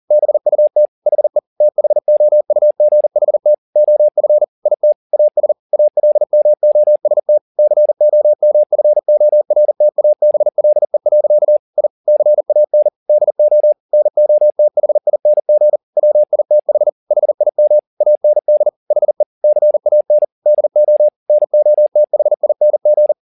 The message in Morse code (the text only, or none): BUT HE THOUGHT OF IT AS ALMOST COMFORTABLE. I CAN DO NOTHING WITH HIM AND HE CAN DO NOTHING